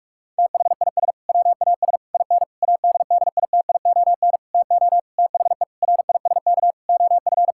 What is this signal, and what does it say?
Morse code, heard as a keyed tone, about 46 wpm, 700 Hz. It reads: THIS WAS IN ADDITION TO THE RISK OF
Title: Sentences from Top 1000 Words - Review Part 6 - 45wpm